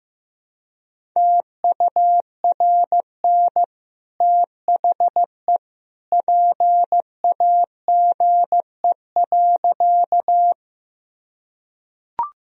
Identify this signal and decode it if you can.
Morse code, heard as a keyed tone, about 15 wpm, 700 Hz. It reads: TURN THE PAGE.
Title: Sentences from Top 1000 Words - Review Part 1 - 15wpm